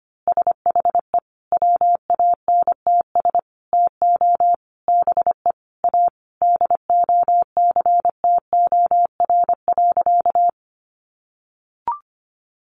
Morse code, heard as a keyed tone, about 25 wpm, 700 Hz. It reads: SHE WANTS TO BE A DOCTOR.
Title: Easy Phrases 25wpm